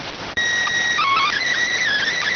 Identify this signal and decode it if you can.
SSTV (slow-scan television), PD 290.